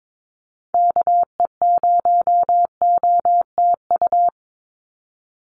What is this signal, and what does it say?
Morse code, heard as a keyed tone, about 22 wpm, 700 Hz. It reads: KE0OTU
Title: USA Call Signs 22wpm